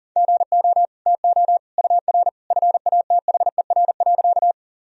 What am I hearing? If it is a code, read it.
Morse code, heard as a keyed tone, about 40 wpm, 700 Hz. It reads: GO TO UR FATHER.
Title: Sentences from Top 300 Words and Abbreviations 40wpm